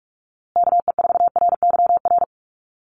Morse code, heard as a keyed tone, about 44 wpm, 700 Hz. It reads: KE4RYR
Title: USA Call Signs Repeated 3x 45wpm